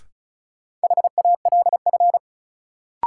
Morse code, heard as a keyed tone, about 36 wpm, 700 Hz. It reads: HALF
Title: Sets of 3 Words - New Word Drill - 35wpm